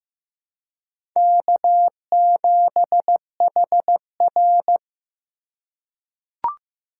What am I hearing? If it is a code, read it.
Morse code, heard as a keyed tone, about 15 wpm, 700 Hz. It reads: K7HR
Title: USA Call Signs - 4x Character Spacing - 15wpm